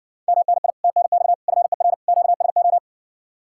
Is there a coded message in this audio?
Yes: Morse code from 0.3 to 2.8 s, heard as a keyed tone, about 61 wpm, 700 Hz. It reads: MNI TNX FER QSO